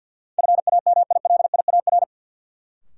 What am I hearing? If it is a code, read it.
Morse code, heard as a keyed tone, about 52 wpm, 700 Hz. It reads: FAMILIAR
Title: Sentences from Top 1000 Words - New Word Drill - 50wpm